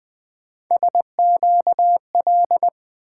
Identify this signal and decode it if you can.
Morse code, heard as a keyed tone, about 20 wpm, 700 Hz. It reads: SQL
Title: Three Letter-Number Combinations - 20wpm